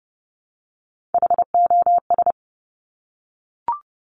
Morse code, heard as a keyed tone, about 30 wpm, 700 Hz. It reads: HOS
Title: Top Three Letter Combinations in Words 30wpm